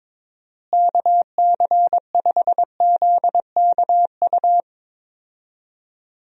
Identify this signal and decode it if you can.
Morse code, heard as a keyed tone, about 22 wpm, 700 Hz. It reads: KC5ZKU
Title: USA Call Signs - 6 Characters - Repeated 3x - 22wpm